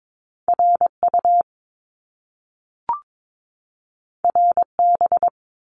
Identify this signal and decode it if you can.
Morse code, heard as a keyed tone, about 22 wpm, 700 Hz. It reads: RU RB
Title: Top Two Letter Combinations in Words 22wpm